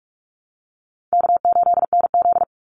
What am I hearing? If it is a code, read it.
Morse code, heard as a keyed tone, about 44 wpm, 700 Hz. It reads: K8NZ
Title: Call Signs - 45wpm